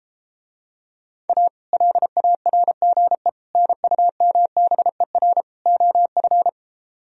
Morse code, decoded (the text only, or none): A LARGE NUMBER OF